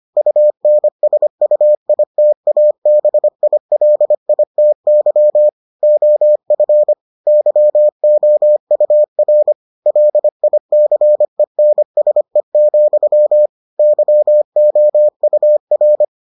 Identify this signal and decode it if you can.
Morse code, heard as a keyed tone, about 25 wpm, 600 Hz. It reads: UNSUITABILITY OF YOUR LICENSE, YOUR